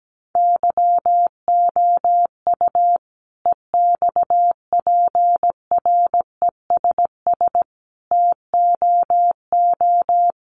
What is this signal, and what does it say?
Morse code, heard as a keyed tone, about 17 wpm, 700 Hz. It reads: YOU EXPRESS TOO
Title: Sentences from Top 1100 Words 17wpm